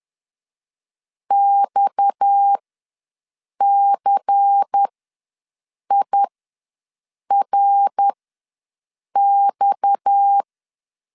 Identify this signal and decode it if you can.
Morse code, heard as a keyed tone, about 10 wpm, spaced as for 6 wpm, 800 Hz. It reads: XCIRX
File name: LEZIONI / 6 PKX0 / 6 PKX ws 10wpm 3706